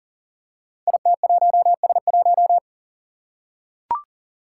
Morse code, heard as a keyed tone, about 38 wpm, 700 Hz. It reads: IT1S1
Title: Five Challenging Letter-Number Combinations 40wpm